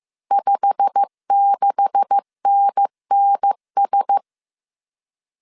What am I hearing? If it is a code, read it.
Morse code, heard as a keyed tone, about 15 wpm, 800 Hz. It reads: 56NNS
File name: CW / DET / 1 9 1500C S C 14wpm 3245